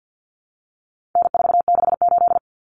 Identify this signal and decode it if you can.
Morse code, heard as a keyed tone, about 50 wpm, 700 Hz. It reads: N468
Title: Common Cut Numbers - 4-Digit 50wpm